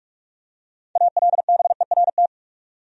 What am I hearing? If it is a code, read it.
Morse code, heard as a keyed tone, about 43 wpm, 700 Hz. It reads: ALBERT
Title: USA Common Names - 45wpm